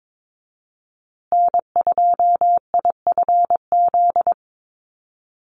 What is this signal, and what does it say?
Morse code, heard as a keyed tone, about 22 wpm, 700 Hz. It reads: N2IFZ
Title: USA Call Signs Repeated 3x 22wpm